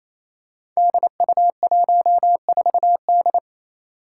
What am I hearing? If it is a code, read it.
Morse code, heard as a keyed tone, about 28 wpm, 700 Hz. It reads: DU14D
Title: Five Challenging Letter-Number Combinations 28wpm